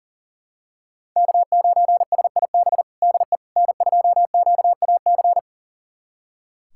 Morse code, heard as a keyed tone, about 40 wpm, 700 Hz. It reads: K9SIB DE N2QAC
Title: USA Callsign DE Callsign - 5 Characters - 40wpm